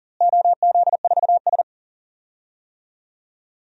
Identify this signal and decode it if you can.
Morse code, heard as a keyed tone, about 40 wpm, 700 Hz. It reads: OZ4S
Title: DXCC-100 Call Signs - 4 Characters - Repeated 3x - 40wpm